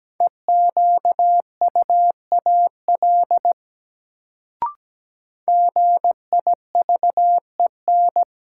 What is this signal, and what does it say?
Morse code, heard as a keyed tone, about 17 wpm, 700 Hz. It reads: EQUAL GIVEN
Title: Sentences from Top 1400 Words - New Word Drill - 17wpm